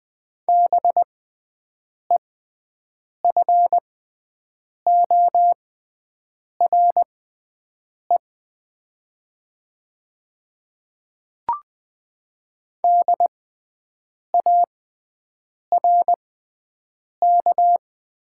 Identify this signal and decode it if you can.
Morse code, heard as a keyed tone, about 20 wpm, spaced as for 7 wpm, 700 Hz. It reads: BEFORE DARK
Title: 6x Character Spacing - Top 500 Words - 20wpm